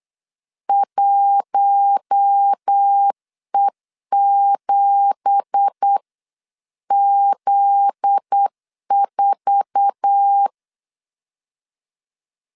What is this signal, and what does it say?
Morse code, heard as a keyed tone, about 8 wpm, 800 Hz. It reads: 1E7Z4
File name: 8wpm